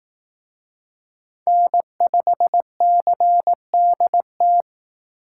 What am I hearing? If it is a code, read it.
Morse code, heard as a keyed tone, about 18 wpm, 700 Hz. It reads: N5CDT
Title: Speed Racing - USA Call Signs 15wpm